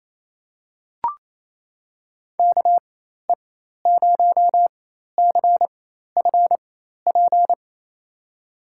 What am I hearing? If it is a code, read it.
Morse code, heard as a keyed tone, about 27 wpm, spaced as for 13 wpm, 700 Hz. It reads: KE0CFP
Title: USA Call Signs - 6 Characters - 4x Character Spacing - 28wpm